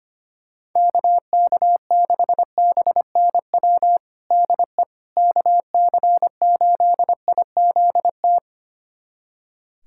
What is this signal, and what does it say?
Morse code, heard as a keyed tone, about 25 wpm, 700 Hz. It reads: KK6BNW DE KC8IZT
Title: USA Callsign DE Callsign 25wpm